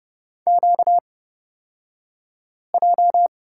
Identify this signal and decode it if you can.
Morse code, heard as a keyed tone, about 29 wpm, spaced as for 5 wpm, 700 Hz. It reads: QJ
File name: Lesson 38d - Letter q - Two Characters 30wpm